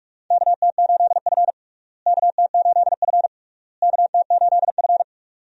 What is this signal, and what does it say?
Morse code, heard as a keyed tone, about 46 wpm, 700 Hz. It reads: KT8F KT8F KT8F
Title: USA Call Signs - 4 Characters - Repeated 3x - 45wpm